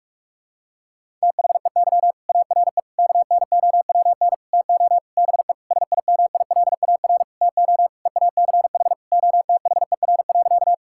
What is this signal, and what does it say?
Morse code, heard as a keyed tone, about 43 wpm, 700 Hz. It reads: THEY ARE KNOWN TO BE SIMILAR TO EACH OTHER.
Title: Sentences from Top 1000 Words - Review Part 5 - 45wpm